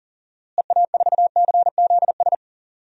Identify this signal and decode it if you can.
Morse code, heard as a keyed tone, about 40 wpm, 700 Hz. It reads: EA4CZS
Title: DXCC-100 Call Signs - 6 Characters - Repeated 3x - 40wpm